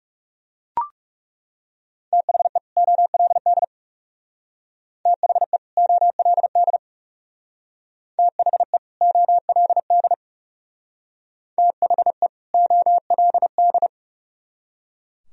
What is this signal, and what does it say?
Morse code, heard as a keyed tone, about 35 wpm, 700 Hz. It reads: THE OLD THE OLD THE OLD THE OLD
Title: Speed-Racing - Sets of 2 Words 30wpm